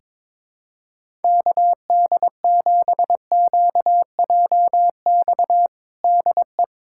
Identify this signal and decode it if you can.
Morse code, heard as a keyed tone, about 22 wpm, 700 Hz. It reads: KD7QJX DE
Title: USA Callsign DE Callsign 22wpm